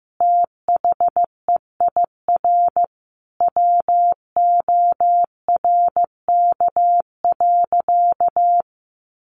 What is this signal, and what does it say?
Morse code, heard as a keyed tone, about 15 wpm, 700 Hz. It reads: THEIR WORK.
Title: Sentences from Top 1400 Words 15wpm